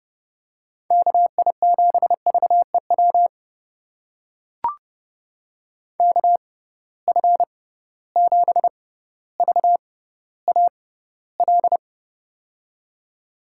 Morse code, heard as a keyed tone, about 30 wpm, spaced as for 15 wpm, 700 Hz. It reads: KI7VEW KF7VAL
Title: USA Call Signs - 6 Characters - 6x Character Spacing - 30wpm